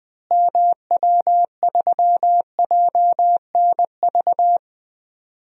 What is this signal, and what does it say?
Morse code, heard as a keyed tone, about 20 wpm, 700 Hz. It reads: MW3JNV
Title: DXCC-100 Call Signs - 6 Characters - Repeated 3x - 20wpm